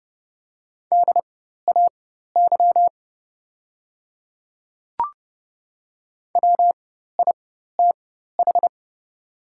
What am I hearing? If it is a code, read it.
Morse code, heard as a keyed tone, about 30 wpm, spaced as for 14 wpm, 700 Hz. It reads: DAY WITH
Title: 4x Character Spacing - Top 100 Words - 30wpm